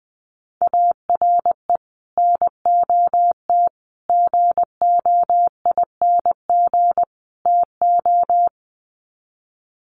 Morse code, heard as a keyed tone, about 19 wpm, 700 Hz. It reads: ARE NOT GOING TO
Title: Sets of 4 Words 20wpm